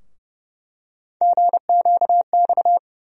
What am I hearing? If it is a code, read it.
Morse code, heard as a keyed tone, about 30 wpm, 700 Hz. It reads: GQX
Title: USA Call Sign Postfixes - 3 Letters 30wpm